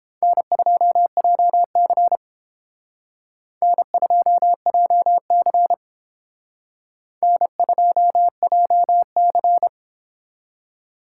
Morse code, heard as a keyed tone, about 29 wpm, 700 Hz. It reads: N2JC N2JC N2JC